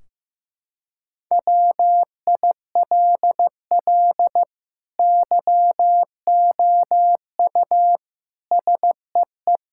Morse code, heard as a keyed tone, about 15 wpm, 700 Hz. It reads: WILL YOU SEE
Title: Sentences from Top 1600 Words 15wpm